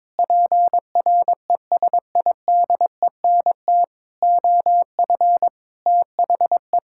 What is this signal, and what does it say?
Morse code, heard as a keyed tone, about 22 wpm, 700 Hz. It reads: PRESIDENT OF THE
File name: practice / 22wpm / Sets of 8 Words - 22wpm